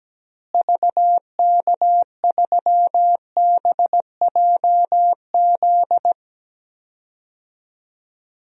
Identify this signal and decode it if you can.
Morse code, heard as a keyed tone, about 17 wpm, 700 Hz. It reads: VK3BJZ
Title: DXCC-100 Call Signs - 6 Characters - Repeated 3x - 17wpm